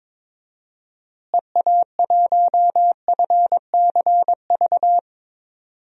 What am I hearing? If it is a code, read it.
Morse code, heard as a keyed tone, about 22 wpm, 700 Hz. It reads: EA1FCV
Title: DXCC-100 Call Signs - 6 Characters - Repeated 2x - 22wpm